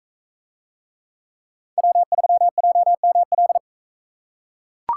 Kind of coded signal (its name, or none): Morse code